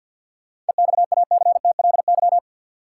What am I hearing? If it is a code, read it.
Morse code, heard as a keyed tone, about 49 wpm, 700 Hz. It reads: EXACTLY
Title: Sentences from Top 900 Words - New Word Drill - 50wpm